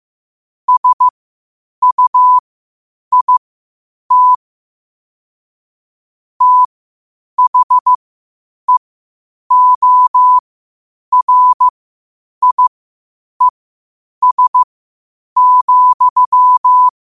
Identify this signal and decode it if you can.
Morse code, heard as a keyed tone, about 15 wpm, spaced as for 8 wpm, 1000 Hz. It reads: SUIT THEORIES,
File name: 8wpm